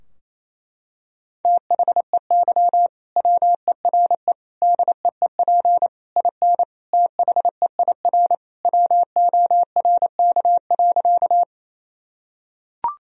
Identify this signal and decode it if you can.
Morse code, heard as a keyed tone, about 28 wpm, 700 Hz. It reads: THEY WERE DEEP IN THEIR WORK.
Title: Sentences from Top 1000 Words - Review Part 3 - 28wpm